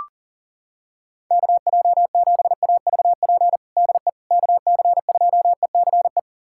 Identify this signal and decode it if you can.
Morse code, heard as a keyed tone, about 40 wpm, 700 Hz. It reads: KJ7AVP DE KC2ECE
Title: USA Callsign DE Callsign 40wpm